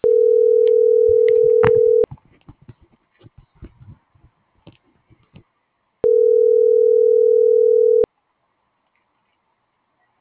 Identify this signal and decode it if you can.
Ringback tone (North America), 440 + 480 Hz.